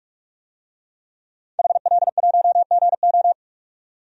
Morse code, heard as a keyed tone, about 42 wpm, 700 Hz. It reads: HL1GO
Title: DXCC-100 Call Signs - 5 Characters - 45wpm